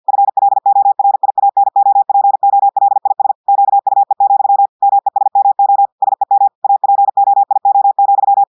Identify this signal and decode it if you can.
Morse code, heard as a keyed tone, about 50 wpm, 800 Hz. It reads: FLORIANOPOLIS QUE, MESMO SEM APOIO,
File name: audios / portugais / portugais texte en clair 1 vitesse 50 mots minutes